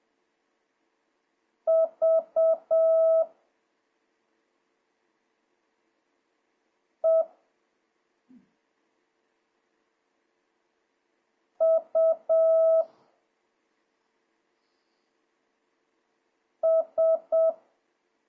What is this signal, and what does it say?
Morse code, heard as a keyed tone, about 7 wpm, 650 Hz. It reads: V E U S
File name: jh8cbh / 2026cw